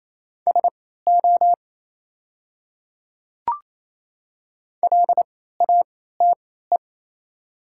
Morse code, heard as a keyed tone, about 27 wpm, spaced as for 16 wpm, 700 Hz. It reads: SO LATE
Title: 3x Character Spacing - Top 300 Words - 28wpm